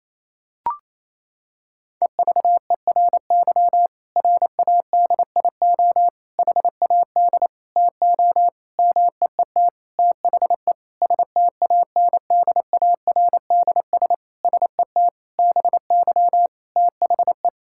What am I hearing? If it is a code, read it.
Morse code, heard as a keyed tone, about 28 wpm, 700 Hz. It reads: EVERY RADIO HAD TO MEET THE STANDARDS SET BY THE
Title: Sentences from Top 1000 Words - Review Part 6 - 28wpm